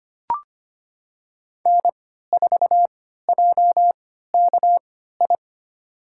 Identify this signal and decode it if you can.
Morse code, heard as a keyed tone, about 25 wpm, spaced as for 14 wpm, 700 Hz. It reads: N4JKI
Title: USA Call Signs - 5 Characters - 3x Character Spacing - 25wpm